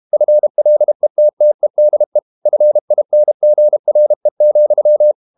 Morse code, heard as a keyed tone, about 32 wpm, 600 Hz. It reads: FLETTEDE FINGRE,
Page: Ferie 32 wpm